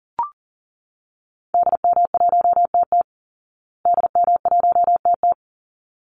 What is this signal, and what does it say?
Morse code, heard as a keyed tone, about 40 wpm, 700 Hz. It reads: DM1TT DM1TT
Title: Call Signs Repeated 3x - 40wpm